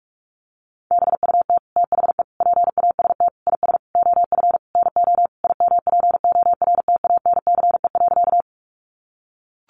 Morse code, heard as a keyed tone, about 44 wpm, 700 Hz. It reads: BUT THE PAST IS OF NO IMPORTANCE.
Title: Sentences from Top 1200 Words 45wpm